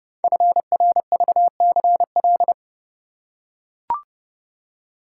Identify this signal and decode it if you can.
Morse code, heard as a keyed tone, about 29 wpm, 700 Hz. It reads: FRVCL
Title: Code Groups - 5 Letters - 30wpm